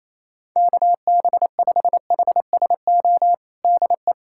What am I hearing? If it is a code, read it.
Morse code, heard as a keyed tone, about 28 wpm, 700 Hz. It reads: KB5HSO DE
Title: USA Callsign DE Callsign 28wpm